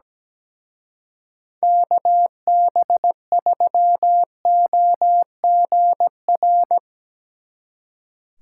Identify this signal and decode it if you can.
Morse code, heard as a keyed tone, about 19 wpm, 700 Hz. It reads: KB3OGR
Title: Speed Racing - USA 2x3 Call Signs 17wpm